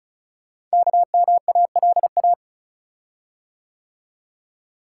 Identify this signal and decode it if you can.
Morse code, heard as a keyed tone, about 36 wpm, 690 Hz. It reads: KMALA
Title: Code Groups - 5 Letters - 35wpm